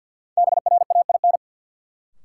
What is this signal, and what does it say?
Morse code, heard as a keyed tone, about 52 wpm, 700 Hz. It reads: BRAIN
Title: Sentences from Top 1000 Words - Review Part 6 - New Word Drill - 50wpm